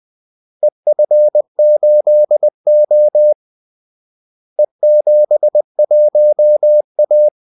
Morse code, heard as a keyed tone, about 20 wpm, 600 Hz. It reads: EF8O E71A
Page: Kaldesignaler 20 wpm